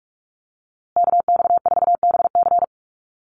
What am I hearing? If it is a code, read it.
Morse code, heard as a keyed tone, about 44 wpm, 700 Hz. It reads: KX4BC
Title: USA Call Signs Repeated 2x 45wpm